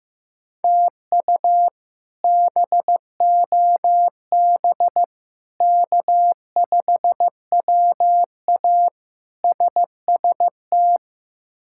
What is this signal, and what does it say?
Morse code, heard as a keyed tone, about 15 wpm, 700 Hz. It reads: TU BOB K5WA SST
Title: K1USN SST QSO Practice - 15wpm